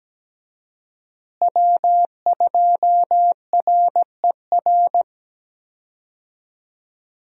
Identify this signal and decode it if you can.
Morse code, heard as a keyed tone, about 17 wpm, 700 Hz. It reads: W2RER
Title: USA Call Signs - 5 Characters - Repeated 2x - 17wpm